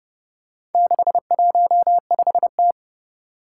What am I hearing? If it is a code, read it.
Morse code, heard as a keyed tone, about 30 wpm, 700 Hz. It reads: 615T
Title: Four Challenging Letter-Number Combinations 30wpm